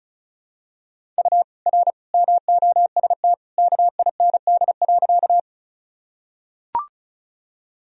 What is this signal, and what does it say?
Morse code, heard as a keyed tone, about 34 wpm, 700 Hz. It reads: U R MOST KIND.
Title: Sentences from Top 200 Words and Abbreviations 35wpm